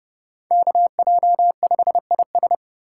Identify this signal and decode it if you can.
Morse code, heard as a keyed tone, about 29 wpm, 700 Hz. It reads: KJ5IS